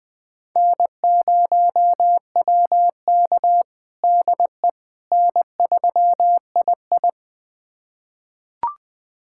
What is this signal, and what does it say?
Morse code, heard as a keyed tone, about 20 wpm, 700 Hz. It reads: N0WK DE N3II
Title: USA Callsign DE Callsign - 4 Characters - 20wpm